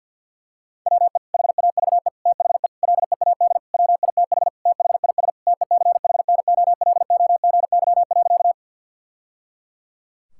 Morse code, heard as a keyed tone, about 50 wpm, 700 Hz. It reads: WE HAVE THE LEAD WITH THIS TECHNOLOGY.
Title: Sentences from Top 1000 Words - Review Part 5 - 50wpm